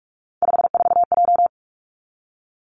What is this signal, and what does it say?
Morse code, heard as a keyed tone, about 46 wpm, 700 Hz. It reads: 54J
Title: Three Challenging Letter-Number Combinations 45wpm